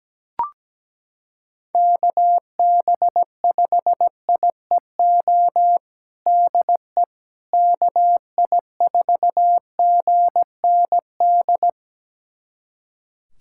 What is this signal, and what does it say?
Morse code, heard as a keyed tone, about 17 wpm, 700 Hz. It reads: KB5IEO DE KI4GND
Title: USA Callsign DE Callsign 17wpm